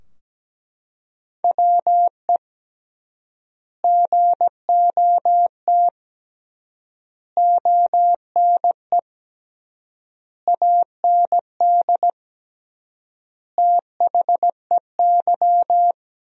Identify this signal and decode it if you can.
Morse code, heard as a keyed tone, about 17 wpm, 700 Hz. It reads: WE GOT ONE AND THEY
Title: 3x Word-Spacing - Sentences from Top 300 Words - 17wpm